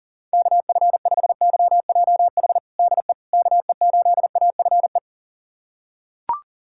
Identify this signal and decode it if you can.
Morse code, heard as a keyed tone, about 40 wpm, 700 Hz. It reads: KF5YJH DE KE8AFE
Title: USA Callsign DE Callsign - 6 Characters - 40wpm